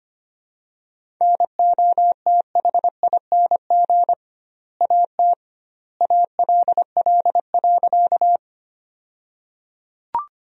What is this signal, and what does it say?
Morse code, heard as a keyed tone, about 25 wpm, 700 Hz. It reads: NOTHING AT ALL.
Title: 2x Word-Spacing - Sentences from Top 500 Words - 25wpm